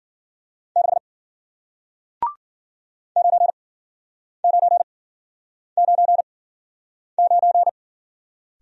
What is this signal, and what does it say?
Morse code, heard as a keyed tone, about 55 wpm, spaced as for 9 wpm, 700 Hz. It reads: 6 9999